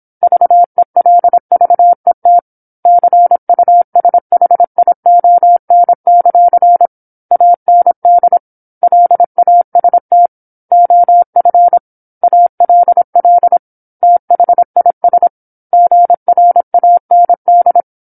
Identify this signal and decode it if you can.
Morse code, heard as a keyed tone, about 27 wpm, 700 Hz. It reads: VELVET CUSHION; AND LAST OF ALL THIS GRAND
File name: 08-Chapter-08 26 WPM